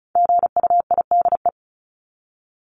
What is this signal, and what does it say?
Morse code, heard as a keyed tone, about 34 wpm, 700 Hz. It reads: GUIDE
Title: Top 501-750 Words 35wpm